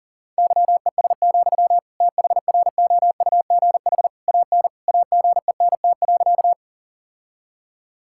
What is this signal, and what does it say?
Morse code, heard as a keyed tone, about 37 wpm, 700 Hz. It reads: YES, THROUGH AN AGENT.
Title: Sentences from Top 1600 Words 40wpm